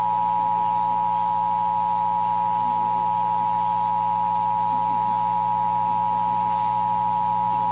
Emergency Alert System (SAME) tones (the 853 and 960 Hz attention tone). It is heard from 1.0 to 7.0 s.
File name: eas / KNCYKBIE / KNCYKBIE